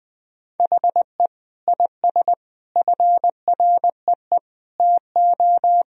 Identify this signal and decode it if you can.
Morse code, heard as a keyed tone, about 21 wpm, 700 Hz. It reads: HE IS FREE TO
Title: Sentences from Top 1000 Words - Review Part 4 - 20wpm